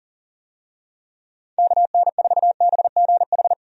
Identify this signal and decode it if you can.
Morse code, heard as a keyed tone, about 40 wpm, 700 Hz. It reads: KN4BGH
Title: USA Call Signs - 6 Characters - 40wpm